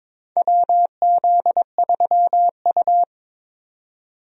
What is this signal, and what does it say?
Morse code, heard as a keyed tone, about 22 wpm, 700 Hz. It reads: WZ3U
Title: USA Call Signs - 4 Characters - 22wpm